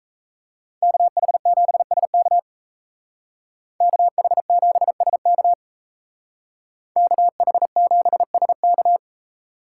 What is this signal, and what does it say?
Morse code, heard as a keyed tone, about 36 wpm, 700 Hz. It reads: KH7SK KH7SK KH7SK